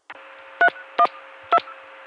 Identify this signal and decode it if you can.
DTMF, heard as telephone touch tones, tones of about 70 ms. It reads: 312